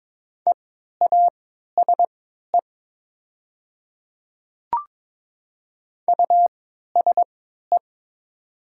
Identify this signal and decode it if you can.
Morse code, heard as a keyed tone, about 22 wpm, spaced as for 13 wpm, 700 Hz. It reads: EASE USE
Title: 3x Character Spacing - Top 300 Words - 22wpm